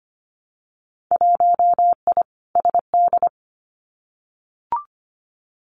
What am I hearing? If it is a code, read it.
Morse code, heard as a keyed tone, about 25 wpm, 700 Hz. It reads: 1I SD
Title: Winter Field Day Exchanges 25wpm